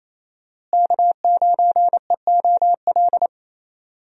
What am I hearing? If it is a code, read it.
Morse code, heard as a keyed tone, about 28 wpm, 700 Hz. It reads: K9EOL
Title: USA Call Signs - 5 Characters - Repeated 2x - 28wpm